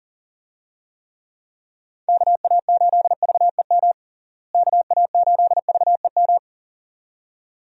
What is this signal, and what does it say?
Morse code, heard as a keyed tone, about 40 wpm, 700 Hz. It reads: KA8VEM KA8VEM
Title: USA Call Signs - 6 Characters - Repeated 2x - 40wpm